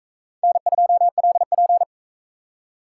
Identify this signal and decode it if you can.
Morse code, heard as a keyed tone, about 42 wpm, 700 Hz. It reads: N2LP